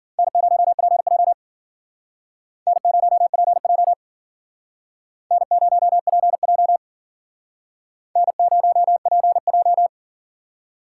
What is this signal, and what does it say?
Morse code, heard as a keyed tone, about 51 wpm, 700 Hz. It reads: N0PJ N0PJ N0PJ N0PJ